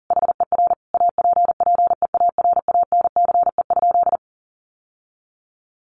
Morse code, heard as a keyed tone, about 40 wpm, 700 Hz. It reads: HER APPEARANCE?
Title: Sentences from Top 1100 Words 40wpm